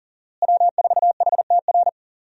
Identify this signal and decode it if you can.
Morse code, heard as a keyed tone, about 40 wpm, 700 Hz. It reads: W4HTR
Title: USA Call Signs - 5 Characters - Repeated 3x - 40wpm